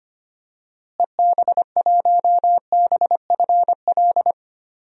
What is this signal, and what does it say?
Morse code, heard as a keyed tone, about 25 wpm, 700 Hz. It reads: EB1BFL
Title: DXCC-100 Call Signs - 6 Characters - Repeated 2x - 25wpm